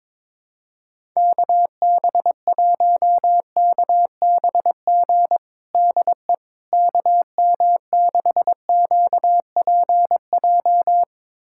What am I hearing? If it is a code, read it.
Morse code, heard as a keyed tone, about 21 wpm, 700 Hz. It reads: KB1KBG DE KM6QPJ
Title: USA Callsign DE Callsign 22wpm